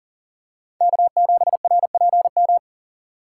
Morse code, heard as a keyed tone, about 40 wpm, 700 Hz. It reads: K7RPM
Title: USA Call Signs - 5 Characters - 3x Character Spacing - 40wpm